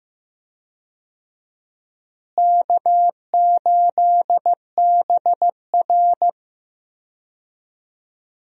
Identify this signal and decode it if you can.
Morse code, heard as a keyed tone, about 15 wpm, 700 Hz. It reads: K8BR
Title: USA Call Signs - 4 Characters - Repeated 3x - 15wpm